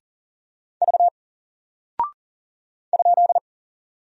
Morse code, heard as a keyed tone, about 40 wpm, 700 Hz. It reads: V ?
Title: Single Character (Letter:Number:Punctuation:Prosign) 40wpm